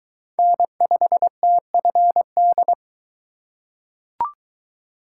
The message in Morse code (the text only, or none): N5TFD